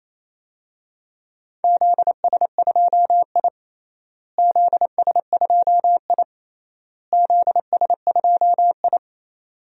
Morse code, heard as a keyed tone, about 27 wpm, 700 Hz. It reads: ZS2I ZS2I ZS2I
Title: DXCC-100 Call Signs - 4 Characters - Repeated 3x - 28wpm